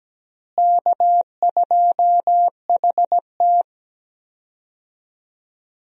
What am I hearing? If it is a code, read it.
Morse code, heard as a keyed tone, about 17 wpm, 700 Hz. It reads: K2HT
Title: USA Call Signs - 4 Characters - Repeated 3x - 17wpm